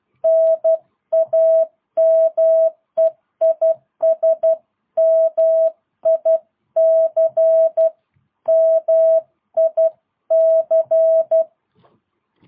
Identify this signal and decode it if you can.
Morse code, heard as a keyed tone, about 12 wpm, 660 Hz. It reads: NAME IS MIC MIC
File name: jh8cbh / 2022cw